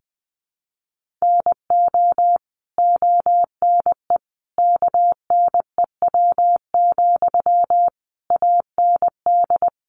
Morse code, heard as a keyed tone, about 20 wpm, 700 Hz. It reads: NO ONE KNEW, AND
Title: Sentences from Top 1400 Words 20wpm